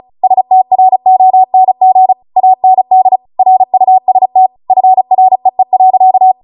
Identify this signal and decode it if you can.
Morse code, heard as a keyed tone, about 36 wpm, 750 Hz. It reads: STRONG AND RUST FREE.